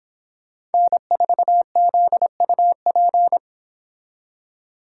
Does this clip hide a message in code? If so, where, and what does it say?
Yes: Morse code from 0.7 to 3.4 s, heard as a keyed tone, about 30 wpm, 700 Hz. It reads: N4ZUP